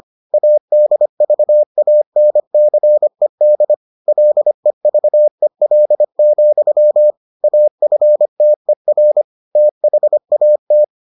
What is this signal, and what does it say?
Morse code, heard as a keyed tone, about 25 wpm, 600 Hz. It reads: ADVANCED LEVEL, AFTER THAT